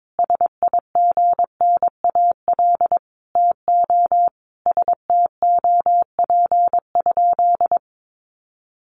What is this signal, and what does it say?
Morse code, heard as a keyed tone, about 22 wpm, 700 Hz. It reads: SIGNAL TO STOP?
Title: Sentences from Top 1400 Words 22wpm